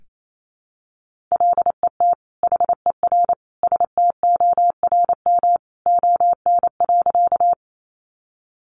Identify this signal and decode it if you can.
Morse code, heard as a keyed tone, about 28 wpm, 700 Hz. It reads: LET HER STORM ON.
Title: Sentences from Top 1500 Words 28wpm